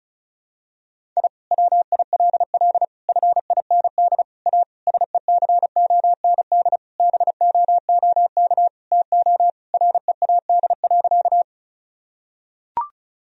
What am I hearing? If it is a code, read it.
Morse code, heard as a keyed tone, about 34 wpm, 700 Hz. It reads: I WILL FIND A SECOND BOOK TO READ.
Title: Sentences from Top 1000 Words - Review Part 1 - 35wpm